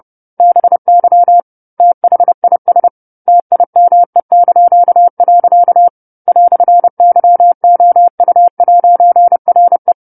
Morse code, heard as a keyed tone, about 30 wpm, 700 Hz. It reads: BY THIS TIME). "YOU'RE